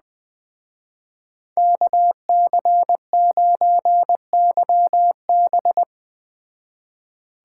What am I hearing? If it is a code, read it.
Morse code, heard as a keyed tone, about 20 wpm, 700 Hz. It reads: KC9YB